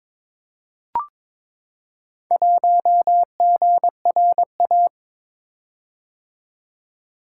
Morse code, heard as a keyed tone, about 22 wpm, 700 Hz. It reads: 1GRA
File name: practice / 22wpm / Four Letter-Number Combinations - 22wpm